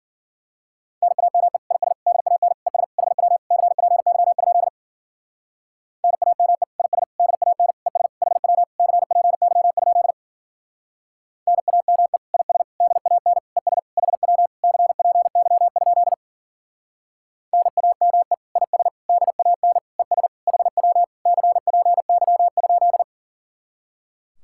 Morse code, heard as a keyed tone, about 50 wpm, 700 Hz. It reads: NAME IS DAN ES HW CPY? NAME IS DAN ES HW CPY? NAME IS DAN ES HW CPY? NAME IS DAN ES HW CPY?
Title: Speed-Racing Common QSO Phrases Encore - 40wpm